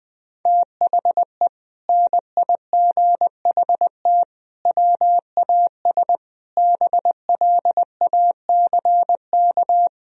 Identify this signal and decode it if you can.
Morse code, heard as a keyed tone, about 20 wpm, 700 Hz. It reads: THE NIGHT WAS BLACK
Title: Sentences from Top 1000 Words - Review Part 2 - 20wpm